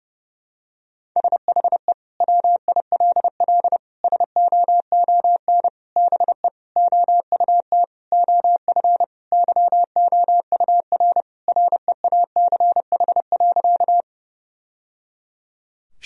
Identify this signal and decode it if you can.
Morse code, heard as a keyed tone, about 30 wpm, 700 Hz. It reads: SHE WILL SOON BE OUT OF YOUR REACH.
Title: Sentences from Top 1000 Words - Review Part 2 - 30wpm